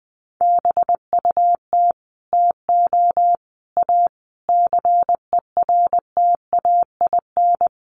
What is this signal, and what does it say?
Morse code, heard as a keyed tone, about 20 wpm, 700 Hz. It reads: BUT TO A CERTAIN
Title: Sentences from Top 1500 Words 20wpm